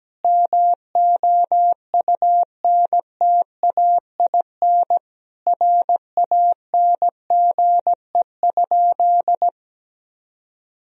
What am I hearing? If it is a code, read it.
Morse code, heard as a keyed tone, about 17 wpm, 700 Hz. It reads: MOUNTAIN RANGE?
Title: Sentences from Top 1000 Words - Review Part 4 - 17wpm